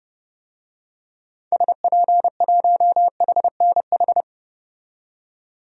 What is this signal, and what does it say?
Morse code, heard as a keyed tone, about 30 wpm, 700 Hz. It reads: SP1HNH
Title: DXCC-100 Call Signs - 6 Characters - 30wpm